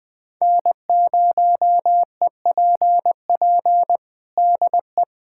Morse code, heard as a keyed tone, about 20 wpm, 700 Hz. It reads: N0EPP DE
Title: USA Callsign DE Callsign - 5 Characters - 20wpm